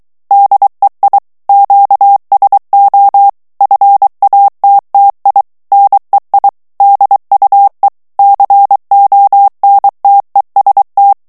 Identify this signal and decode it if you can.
Morse code, heard as a keyed tone, about 23 wpm, 800 Hz. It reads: DEI QSO FATTI NEI DUE CONTEST
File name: ITALIEN texte clair 2 vitesse 22 mots minute